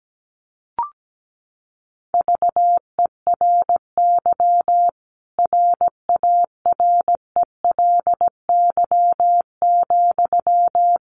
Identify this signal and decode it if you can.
Morse code, heard as a keyed tone, about 17 wpm, 700 Hz. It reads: VERY RARELY,
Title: Sentences from Top 1400 Words 17wpm